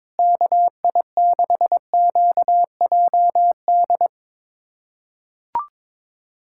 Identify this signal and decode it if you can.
Morse code, heard as a keyed tone, about 22 wpm, 700 Hz. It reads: KI6QJD
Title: USA Callsign DE Callsign - 6 Characters - 22wpm